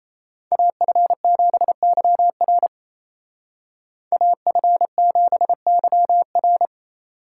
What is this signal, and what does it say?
Morse code, heard as a keyed tone, about 33 wpm, 700 Hz. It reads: AF7YR AF7YR
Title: Speed Racing - USA Call Signs 28wpm